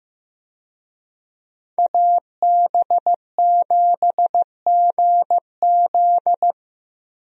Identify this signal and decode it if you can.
Morse code, heard as a keyed tone, about 15 wpm, 700 Hz. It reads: AB7GZ
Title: USA Call Signs - 5 Characters - Repeated 2x - 15wpm